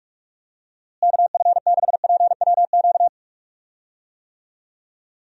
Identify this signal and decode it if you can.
Morse code, heard as a keyed tone, about 42 wpm, 700 Hz. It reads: KF6PWQ
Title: USA Call Signs - 6 Characters - Repeated 3x - 45wpm